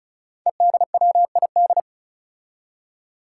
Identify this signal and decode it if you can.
Morse code, heard as a keyed tone, about 36 wpm, 700 Hz. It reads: EDWID
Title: Five Challenging Letter-Number Combinations 35wpm